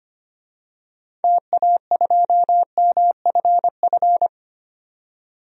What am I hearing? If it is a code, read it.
Morse code, heard as a keyed tone, about 25 wpm, 700 Hz. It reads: TA2MFF
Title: DXCC-100 Call Signs - 6 Characters - 25wpm